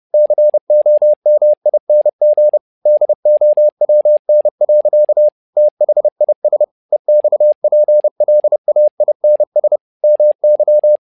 Morse code, heard as a keyed tone, about 30 wpm, 600 Hz. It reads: COMING DOWN. THIS EXPLAINS MY
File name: learning center / tower trouble / tower trouble 30wpm